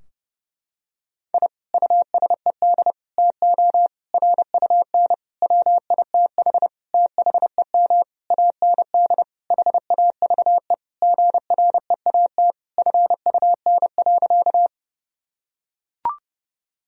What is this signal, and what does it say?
Morse code, heard as a keyed tone, about 32 wpm, 700 Hz. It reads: I USED TO RUN WITH THEM AND HAVE GREAT FUN.
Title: Sentences from Top 1000 Words - Review Part 4 - 30wpm